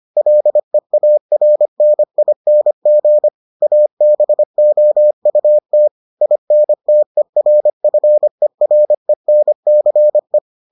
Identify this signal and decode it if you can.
Morse code, heard as a keyed tone, about 25 wpm, 600 Hz. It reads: LEARNING ABOUT INTERFERENCE